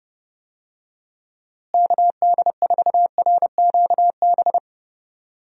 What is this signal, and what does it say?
Morse code, heard as a keyed tone, about 30 wpm, 700 Hz. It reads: KD4RQB
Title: USA Call Signs - 6 Characters - Repeated 3x - 30wpm